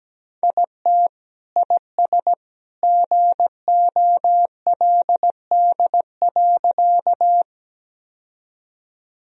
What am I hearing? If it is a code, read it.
Morse code, heard as a keyed tone, about 17 wpm, 700 Hz. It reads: IT IS GOLD.
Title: Sentences from Top 1000 Words - Review Part 3 - 17wpm